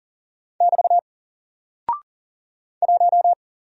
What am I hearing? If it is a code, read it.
Morse code, heard as a keyed tone, about 40 wpm, 700 Hz. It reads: <BT> 1
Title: Single Character (Letter:Number:Punctuation:Prosign) Alt 40wpm